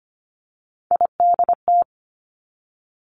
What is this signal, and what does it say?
Morse code, heard as a keyed tone, about 25 wpm, 700 Hz. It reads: IDT
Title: Three Challenging Letter-Number Combinations 25wpm